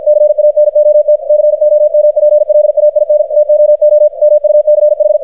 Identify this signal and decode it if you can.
Morse code, heard as a keyed tone, about 53 wpm, 600 Hz, with radio noise. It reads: CANOT COMPLAIN TOO MUCH
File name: archived / CRSnet